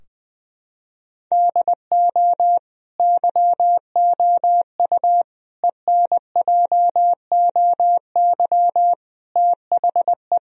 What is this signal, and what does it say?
Morse code, heard as a keyed tone, about 20 wpm, 700 Hz. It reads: DO YOU ENJOY THE
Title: Sentences from Top 1000 Words - Review Part 4 - 20wpm